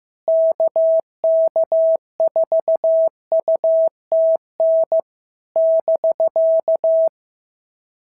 Morse code, heard as a keyed tone, about 14 wpm, 650 Hz. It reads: KK4UTN <BK>
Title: SOTA QSO Practice 15wpm